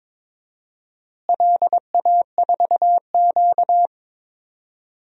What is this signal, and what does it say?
Morse code, heard as a keyed tone, about 22 wpm, 700 Hz. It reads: LA4Q
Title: DXCC-100 Call Signs - 4 Characters - Repeated 2x - 22wpm